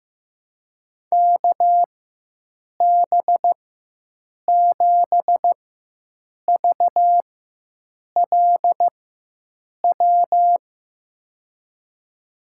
Morse code, heard as a keyed tone, about 15 wpm, spaced as for 7 wpm, 700 Hz. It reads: KB7VLW
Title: USA Call Signs - 6 Characters - 4x Character Spacing - 15wpm